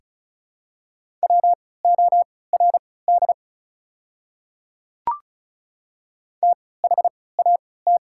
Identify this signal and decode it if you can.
Morse code, heard as a keyed tone, about 34 wpm, spaced as for 20 wpm, 690 Hz. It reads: WORD THAT
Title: 3x Character Spacing - Top 100 Words - 35wpm